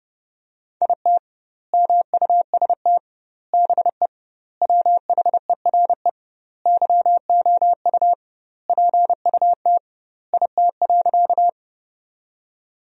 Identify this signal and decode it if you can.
Morse code, heard as a keyed tone, about 30 wpm, 700 Hz. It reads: IT MUST BE WHERE YOU PUT IT.
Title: 2x Word-Spacing - Sentences from Top 200 Words - 30wpm